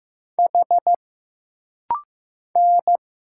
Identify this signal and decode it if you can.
Morse code, heard as a keyed tone, about 15 wpm, 700 Hz. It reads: H N
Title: Single Letters - Rapid-Fire 15wpm